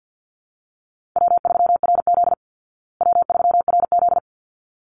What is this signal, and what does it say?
Morse code, heard as a keyed tone, about 50 wpm, 700 Hz. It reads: W3RZ W3RZ
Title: Easy Call Signs Repeated 2x - 50wpm